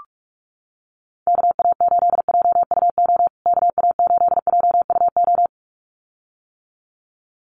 Morse code, heard as a keyed tone, about 44 wpm, 700 Hz. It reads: KA8JUO KA8JUO
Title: USA Call Signs Repeated 2x 45wpm